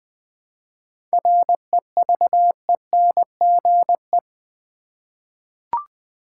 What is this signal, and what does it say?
Morse code, heard as a keyed tone, about 20 wpm, 700 Hz. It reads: REVENGE
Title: 7 Letter Words - First 3 Letters Given - 20wpm